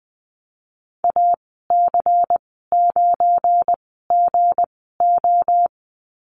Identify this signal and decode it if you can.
Morse code, heard as a keyed tone, about 20 wpm, spaced as for 15 wpm, 700 Hz. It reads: AC9GO
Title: 200 Lesson 32g - Letter k - Call Signs 20wpm